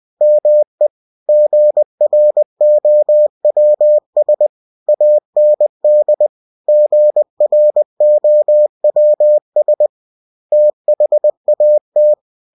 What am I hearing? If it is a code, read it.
Morse code, heard as a keyed tone, about 20 wpm, 600 Hz. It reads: ME GROWS AND GROWS THAT
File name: cw-books / winnie-the-pooh / winnie-20wpm-page-0068